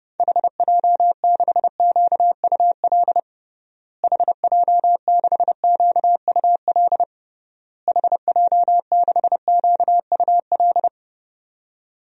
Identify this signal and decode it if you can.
Morse code, heard as a keyed tone, about 30 wpm, 700 Hz. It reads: HJ6QUL HJ6QUL HJ6QUL
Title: DXCC-100 Call Signs - 6 Characters - Repeated 3x - 30wpm